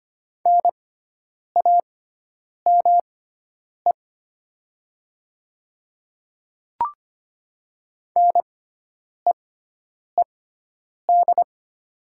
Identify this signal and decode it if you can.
Morse code, heard as a keyed tone, about 25 wpm, spaced as for 9 wpm, 690 Hz. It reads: NAME NEED
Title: 6x Character Spacing - Top 200 Words - 25wpm